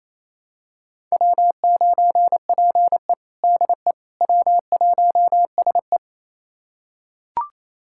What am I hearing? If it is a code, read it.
Morse code, heard as a keyed tone, about 28 wpm, 700 Hz. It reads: W9PE DE W1SE
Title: USA Callsign DE Callsign - 4 Characters - 28wpm